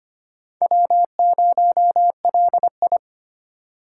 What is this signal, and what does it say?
Morse code, heard as a keyed tone, about 25 wpm, 700 Hz. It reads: W0LI
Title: USA Call Signs - 4 Characters - 2x Character Spacing - 25wpm